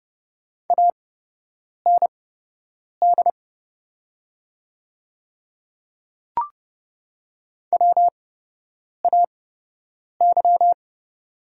Morse code, heard as a keyed tone, about 31 wpm, spaced as for 8 wpm, 700 Hz. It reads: AND WAY
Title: 8x Character Spacing - Top 100 Words - 30wpm